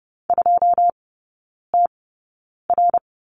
Morse code, heard as a keyed tone, about 30 wpm, spaced as for 9 wpm, 700 Hz. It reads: 2TR
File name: 081 Lesson 14e - Number 2 - Three Characters 30wpm